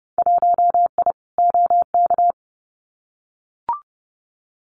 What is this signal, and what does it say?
Morse code, heard as a keyed tone, about 30 wpm, 700 Hz. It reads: 1I OK
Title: Winter Field Day Exchanges Encore 30wpm